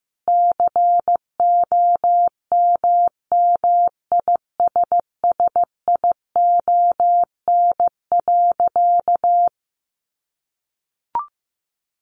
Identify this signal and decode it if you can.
Morse code, heard as a keyed tone, about 15 wpm, 700 Hz. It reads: COMMISSION.
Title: Sentences from Top 1400 Words 15wpm